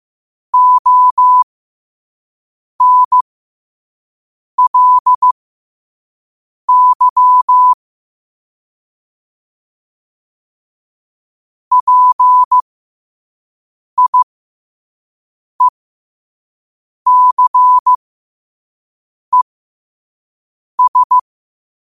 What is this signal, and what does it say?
Morse code, heard as a keyed tone, about 15 wpm, spaced as for 5 wpm, 1000 Hz. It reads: ONLY PIECES
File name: qotdfiles / 5wpm